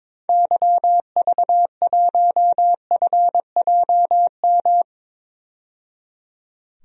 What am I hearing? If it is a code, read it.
Morse code, heard as a keyed tone, about 22 wpm, 700 Hz. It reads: YV1FJM
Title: DXCC-100 Call Signs - 6 Characters - 22wpm